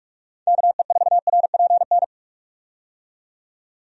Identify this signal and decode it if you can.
Morse code, heard as a keyed tone, about 46 wpm, 700 Hz. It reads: KE4RPN
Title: USA Call Signs - 6 Characters - 45wpm